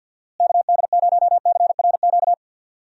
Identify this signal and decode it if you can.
Morse code, heard as a keyed tone, about 52 wpm, 700 Hz. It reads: KD0CRQ
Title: USA Call Signs - 6 Characters - Repeated 3x - 50wpm